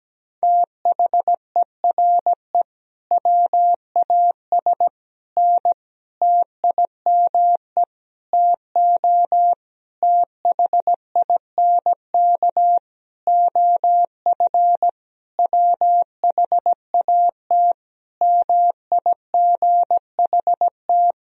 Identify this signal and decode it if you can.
Morse code, heard as a keyed tone, about 17 wpm, 700 Hz. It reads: THERE WAS N TIME TO THINK OF WHAT MIGHT
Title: Sentences from Top 400 Words and Abbreviations 17wpm